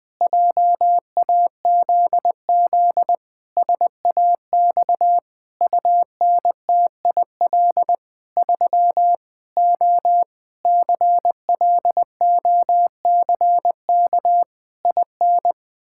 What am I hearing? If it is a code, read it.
Morse code, heard as a keyed tone, about 20 wpm, 700 Hz. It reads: JAZZ SAX UNTIL 3 O CLOCK IN
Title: Pangram Challenge v2 - 20wpm